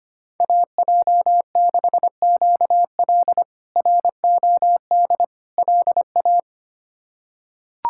Morse code, heard as a keyed tone, about 25 wpm, 700 Hz. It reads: AJ6QL ROD LA
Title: K1USN SST QSO Practice - 25wpm